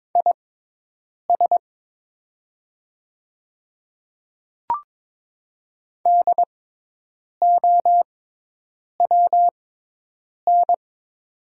Morse code, heard as a keyed tone, about 23 wpm, spaced as for 8 wpm, 700 Hz. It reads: IS DOWN
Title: 6x Character Spacing - Top 100 Words - 22wpm